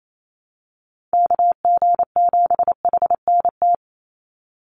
Morse code, heard as a keyed tone, about 28 wpm, 700 Hz. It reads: KG7HNT
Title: USA Call Signs 28wpm